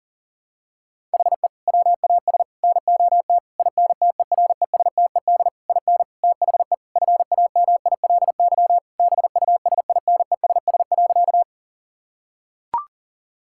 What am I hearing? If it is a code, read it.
Morse code, heard as a keyed tone, about 40 wpm, 700 Hz. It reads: HE WAS NOT INTERESTED IN THE FAMILY BUSINESS.
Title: Sentences from Top 1000 Words - Review Part 6 - 40wpm